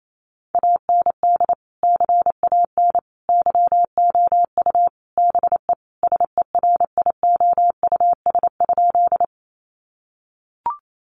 Morse code, heard as a keyed tone, about 28 wpm, 700 Hz. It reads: AND CAN YOU BE SERIOUS?
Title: Sentences from Top 1100 Words 28wpm